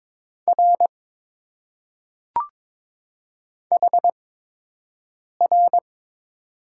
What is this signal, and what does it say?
Morse code, heard as a keyed tone, about 22 wpm, spaced as for 6 wpm, 700 Hz. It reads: R HR
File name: Top 90 QSO Elements - 8x Character Spacing - 22wpm